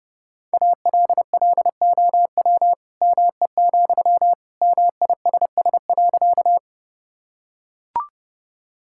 Morse code, heard as a keyed tone, about 30 wpm, 700 Hz. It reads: ALLOW ME, MISS.
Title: Sentences from Top 1000 Words - Review Part 5 - 30wpm